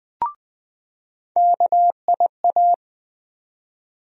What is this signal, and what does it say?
Morse code, heard as a keyed tone, about 20 wpm, 700 Hz. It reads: KIA
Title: USA Call Sign Postfixes - 3 Letters 20wpm